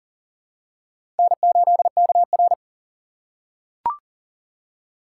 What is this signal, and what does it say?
Morse code, heard as a keyed tone, about 40 wpm, 700 Hz. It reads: N8KR
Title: USA Call Signs - 4 Characters - 2x Character Spacing - 40wpm